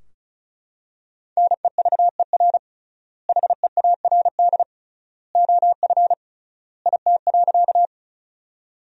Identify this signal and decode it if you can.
Morse code, heard as a keyed tone, about 36 wpm, 700 Hz. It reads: NEVER HEARD OF IT.
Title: 3x Word-Spacing - Sentences from Top 400 Words - 35wpm